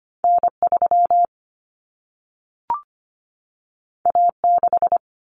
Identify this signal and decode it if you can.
Morse code, heard as a keyed tone, about 25 wpm, 700 Hz. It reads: N3 A6
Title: Common Cut Numbers - 2-Digit 25wpm